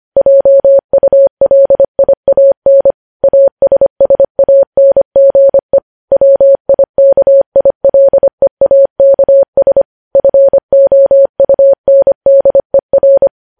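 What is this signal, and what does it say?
Morse code, heard as a keyed tone, about 25 wpm, 550 Hz. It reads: JULIAN ASSANGE WIKILEAKS FOUNDER
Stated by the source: Morse code news headlines sent at 25 wpm.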